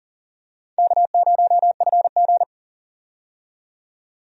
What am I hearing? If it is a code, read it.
Morse code, heard as a keyed tone, about 40 wpm, 700 Hz. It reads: K0FG
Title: USA Call Signs - 4 Characters - 40wpm